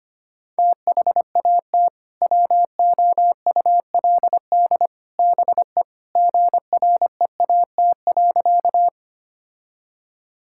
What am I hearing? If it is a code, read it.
Morse code, heard as a keyed tone, about 25 wpm, 700 Hz. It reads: THAT WOULD BE GREAT.
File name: practice / 25wpm / Sentences with 3-Word Phrases - 25wpm